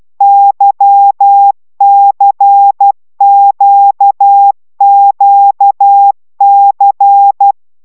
Morse code, heard as a keyed tone, about 12 wpm, 800 Hz. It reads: YCQQC
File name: taviro tanulas / cw 75 15 10